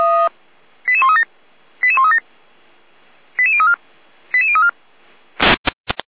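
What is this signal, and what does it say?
Five-tone selective call (ZVEI), heard as quick runs of five tones, tones of about 70 ms. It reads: 88137 88137 88034 88034
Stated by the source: ZVEI five-tone call received on SDR by feuerwehr-orlamuende fire brigade.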